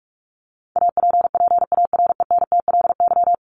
Morse code, heard as a keyed tone, about 44 wpm, 700 Hz. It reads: APPARENTLY
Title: LY Suffix - Top 100 Words with Base Words 45wpm